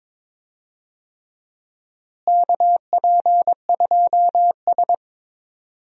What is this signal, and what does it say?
Morse code, heard as a keyed tone, about 21 wpm, 700 Hz. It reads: KP2S
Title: DXCC-100 Call Signs - 4 Characters - Repeated 3x - 22wpm